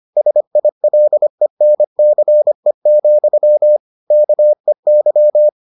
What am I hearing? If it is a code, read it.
Morse code, heard as a keyed tone, about 25 wpm, 600 Hz. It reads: SILENCE, KEY